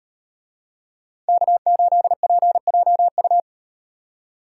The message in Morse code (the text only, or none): K8PJU